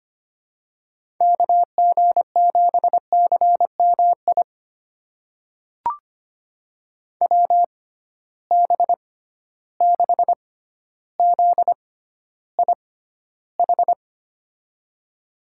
Morse code, heard as a keyed tone, about 25 wpm, spaced as for 13 wpm, 700 Hz. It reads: KG7CMI WB6ZIH
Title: USA Call Signs - 6 Characters - 6x Character Spacing - 25wpm